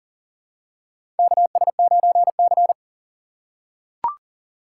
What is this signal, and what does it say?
Morse code, heard as a keyed tone, about 40 wpm, 700 Hz. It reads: KS9C
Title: USA Call Signs - 4 Characters - 2x Character Spacing - 40wpm